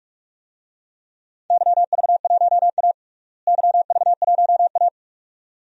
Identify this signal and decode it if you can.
Morse code, heard as a keyed tone, about 46 wpm, 700 Hz. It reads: YV1A YV1A
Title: DXCC-100 Call Signs - 4 Characters - Repeated 2x - 45wpm